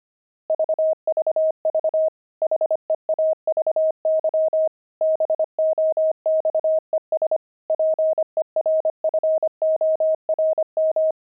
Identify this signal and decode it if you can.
Morse code, heard as a keyed tone, about 25 wpm, 630 Hz. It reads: VVV HEAVY BOXES PERFORM
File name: pangram-fs-25-cs-25-heavy